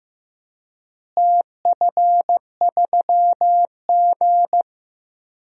Morse code, heard as a keyed tone, about 15 wpm, 700 Hz. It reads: TF3G
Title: DXCC-100 Call Signs - 4 Characters - Repeated 2x - 15wpm